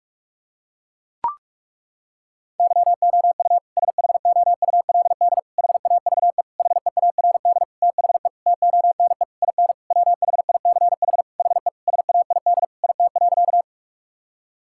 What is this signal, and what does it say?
Morse code, heard as a keyed tone, about 41 wpm, 700 Hz. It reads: YOU SHOULD HAVE HEARD THE TONE IN WHICH HE SAID IT.
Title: Sentences from Top 1600 Words 45wpm